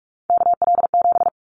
Morse code, heard as a keyed tone, about 46 wpm, 700 Hz. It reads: KL7
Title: Callsign Prefixes for Top 100 DXCC - 45wpm